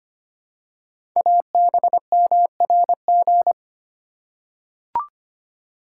Morse code, heard as a keyed tone, about 25 wpm, 700 Hz. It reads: ABMRG
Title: Code Groups - 5 Letters - 25wpm